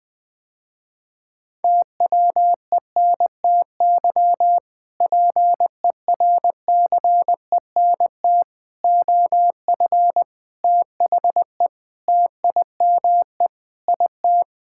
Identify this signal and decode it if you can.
Morse code, heard as a keyed tone, about 21 wpm, 700 Hz. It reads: TWENTY PERCENT OF THE TIME IT
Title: Sentences with 2-Word Phrases - Spoken Beginning - 20wpm